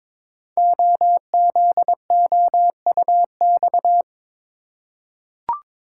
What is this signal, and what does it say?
Morse code, heard as a keyed tone, about 22 wpm, 700 Hz. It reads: OZOUX
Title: Code Groups - 5 Letter:Numbers - 22wpm